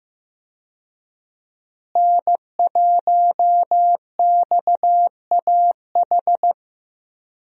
Morse code, heard as a keyed tone, about 15 wpm, 700 Hz. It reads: N1XAH
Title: USA Call Signs - 5 Characters - Repeated 2x - 15wpm